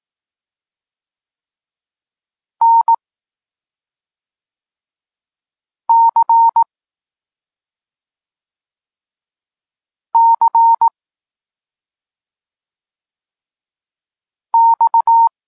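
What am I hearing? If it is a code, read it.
Morse code, heard as a keyed tone, about 18 wpm, 920 Hz. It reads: N C C X
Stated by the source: Polish Morse lesson teaching the letter C, from SP9MOA Niepołomicki Klub Krótkofalowców.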